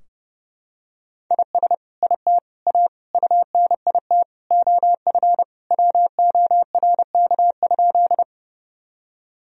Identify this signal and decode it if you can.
Morse code, heard as a keyed tone, about 30 wpm, 700 Hz. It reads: IS IT A UNIT OF WORK?
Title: Sentences from Top 1000 Words - Review Part 2 - 30wpm